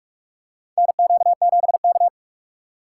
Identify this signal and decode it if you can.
Morse code, heard as a keyed tone, about 46 wpm, 700 Hz. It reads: NQ7K
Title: USA Call Signs - 4 Characters - 4x Character Spacing - 45wpm